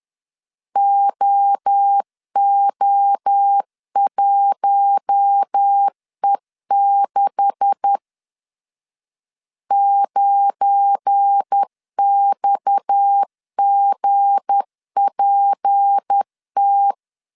Morse code, heard as a keyed tone, about 10 wpm, 800 Hz. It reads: OO1E6 9XGPT
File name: CW / DET / 1 9 1500C S B 10wpm 4439